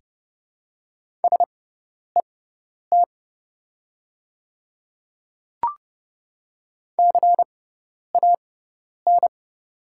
Morse code, heard as a keyed tone, about 30 wpm, spaced as for 10 wpm, 700 Hz. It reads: SET CAN
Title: 6x Character Spacing - Top 500 Words - 30wpm